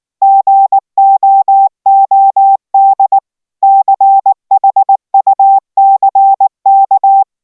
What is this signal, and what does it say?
Morse code, heard as a keyed tone, about 19 wpm, 780 Hz. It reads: GOOD CHUCK